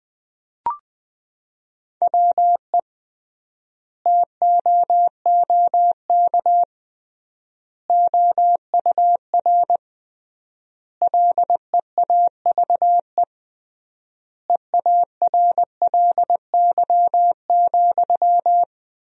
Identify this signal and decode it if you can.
Morse code, heard as a keyed tone, about 20 wpm, 700 Hz. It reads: WE TOOK OUR LEAVE EARLY,
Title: 3x Word-Spacing - Sentences from Top 400 Words - 20wpm